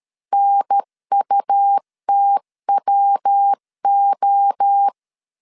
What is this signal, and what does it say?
Morse code, heard as a keyed tone, about 13 wpm, 800 Hz. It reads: NUTWO
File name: CW / DET / 1 3 1500C S A 12wpm 2905